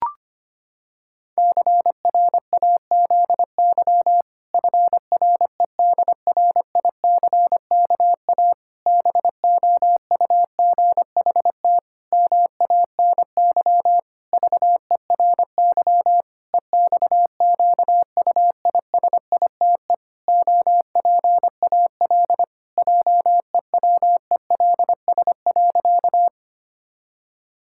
Morse code, heard as a keyed tone, about 25 wpm, 700 Hz. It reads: CRAZY FREDRICKA BOUGHT MANY VERY EXQUISITE OPAL JEWELS.
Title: Pangram Challenge v2 - 25wpm